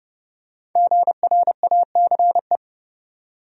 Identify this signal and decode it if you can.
Morse code, heard as a keyed tone, about 30 wpm, 700 Hz. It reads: GRACE
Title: USA Common Names - 30wpm